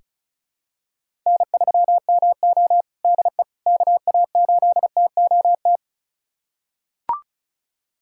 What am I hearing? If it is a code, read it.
Morse code, heard as a keyed tone, about 36 wpm, 700 Hz. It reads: N3MO DE KA8TOT
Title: USA Callsign DE Callsign 35wpm